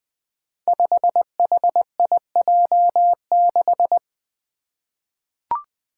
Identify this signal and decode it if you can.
Morse code, heard as a keyed tone, about 20 wpm, 700 Hz. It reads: 5HIJ6
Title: Five Challenging Letter-Number Combinations 20wpm